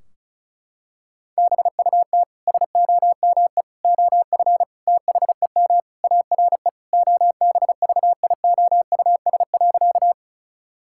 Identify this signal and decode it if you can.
Morse code, heard as a keyed tone, about 36 wpm, 700 Hz. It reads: BUT SOME OF THEM ARE OBVIOUS.
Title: Sentences from Top 1600 Words 35wpm